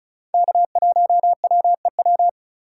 Morse code, heard as a keyed tone, about 34 wpm, 700 Hz. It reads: K1WEW
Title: USA Call Signs - 4x Character Spacing - 35wpm